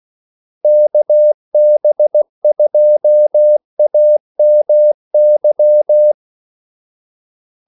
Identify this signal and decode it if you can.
Morse code, heard as a keyed tone, about 16 wpm, 600 Hz. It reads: KB2AMY
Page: Kaldesignaler 16 wpm